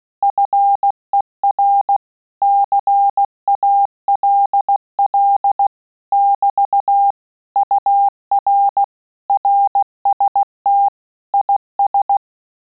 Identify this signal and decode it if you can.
Morse code, heard as a keyed tone, about 16 wpm, 800 Hz. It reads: FER CALL <BT> UR RST IS
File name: Telegraphy-group / QSO CW 15 PPM